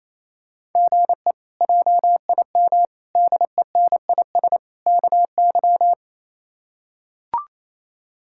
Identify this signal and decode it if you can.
Morse code, heard as a keyed tone, about 28 wpm, 700 Hz. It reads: GE JIM DENIS KY
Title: K1USN SST QSO Practice - 28wpm